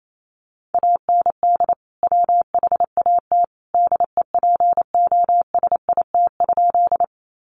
Morse code, heard as a keyed tone, about 28 wpm, 700 Hz. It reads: AND WHAT DEPOSIT?
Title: Sentences from Top 1300 Words 28wpm